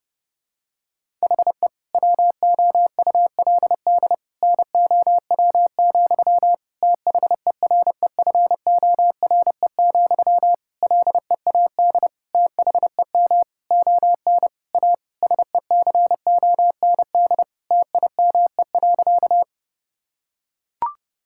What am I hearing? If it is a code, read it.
Morse code, heard as a keyed tone, about 30 wpm, 700 Hz. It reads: HE WOULD NOW, THEREFORE, LEAD THEM ON A SECOND TIME.
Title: Sentences from Top 1000 Words - Review Part 5 - 30wpm